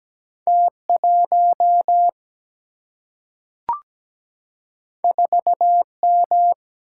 Morse code, heard as a keyed tone, about 17 wpm, 700 Hz. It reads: T 1 4 M
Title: Single-Letter Single-Number 17wpm